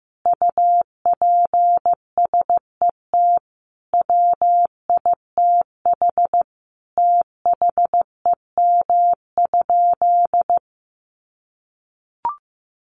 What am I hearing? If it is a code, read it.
Morse code, heard as a keyed tone, about 15 wpm, 700 Hz. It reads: UPSET WITH THEM?
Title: Sentences from Top 800 Words 15wpm